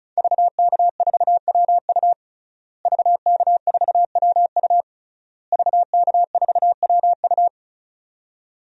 Morse code, heard as a keyed tone, about 34 wpm, 700 Hz. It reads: VK4WU VK4WU VK4WU
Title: DXCC-100 Call Signs - 5 Characters - Repeated 3x - 35wpm